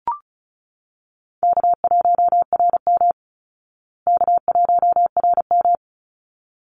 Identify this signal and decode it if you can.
Morse code, heard as a keyed tone, about 34 wpm, 700 Hz. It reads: K1RM K1RM
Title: Easy Call Signs Repeated 2x - 35wpm